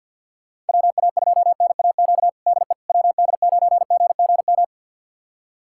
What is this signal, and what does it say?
Morse code, heard as a keyed tone, about 48 wpm, 700 Hz. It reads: WA2NAQ DE WD9GGM
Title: USA Callsign DE Callsign - 6 Characters - 50wpm